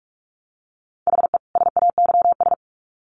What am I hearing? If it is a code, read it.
Morse code, heard as a keyed tone, about 44 wpm, 700 Hz. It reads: HE SAYS
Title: Sets of 2 Words 45wpm